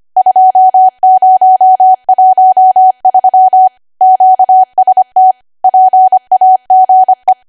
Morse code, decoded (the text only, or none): 2013 QST PAGE